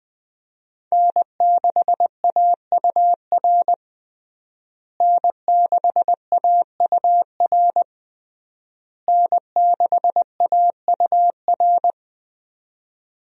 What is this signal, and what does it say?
Morse code, heard as a keyed tone, about 21 wpm, 700 Hz. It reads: N6AUR N6AUR N6AUR
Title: USA Call Signs - 5 Characters - Repeated 3x - 20wpm